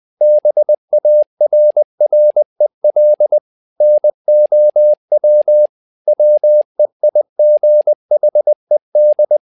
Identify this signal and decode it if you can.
Morse code, heard as a keyed tone, about 20 wpm, 600 Hz. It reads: BARREL NOW WEIGHED
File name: tower trouble 20wpm